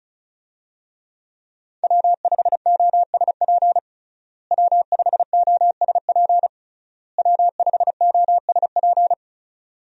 Morse code, heard as a keyed tone, about 36 wpm, 700 Hz. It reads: W5OSP W5OSP W5OSP
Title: USA Call Signs - 5 Characters - Repeated 3x - 35wpm